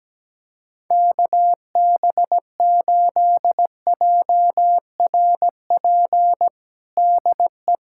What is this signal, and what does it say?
Morse code, heard as a keyed tone, about 17 wpm, 700 Hz. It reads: KB8JRP DE
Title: USA Callsign DE Callsign - 6 Characters - 17wpm